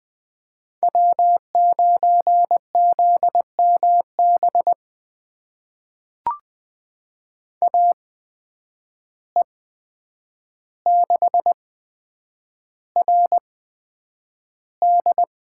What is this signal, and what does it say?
Morse code, heard as a keyed tone, about 20 wpm, spaced as for 9 wpm, 700 Hz. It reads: W9ZMB AE6RD
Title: USA Call Signs - 5 Characters - 8x Character Spacing - 20wpm